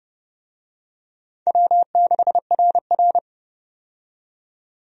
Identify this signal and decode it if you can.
Morse code, heard as a keyed tone, about 32 wpm, 700 Hz. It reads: W6RR
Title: USA Call Signs - 4 Characters - 30wpm